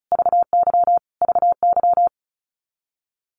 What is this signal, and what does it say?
Morse code, heard as a keyed tone, about 34 wpm, 700 Hz. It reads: VY VY
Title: Abbreviations Repeated 3x - 35wpm